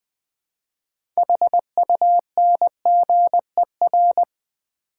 Morse code, heard as a keyed tone, about 20 wpm, 700 Hz. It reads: HUNGER
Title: Build A Word - 2 Letters At A Time - 6-Letter Words - 20wpm